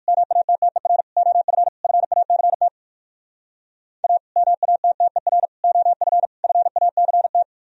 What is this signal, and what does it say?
Morse code, heard as a keyed tone, about 62 wpm, 700 Hz. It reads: MATTER OF FACT A MATTER OF FACT